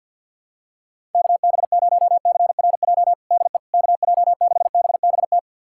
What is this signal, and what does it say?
Morse code, heard as a keyed tone, about 50 wpm, 700 Hz. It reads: KB0CRJ DE KJ6BBT
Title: USA Callsign DE Callsign - 6 Characters - 50wpm